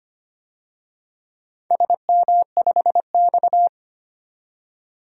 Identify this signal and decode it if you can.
Morse code, heard as a keyed tone, about 25 wpm, 700 Hz. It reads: SM5X
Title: DXCC-100 Call Signs - 4 Characters - 25wpm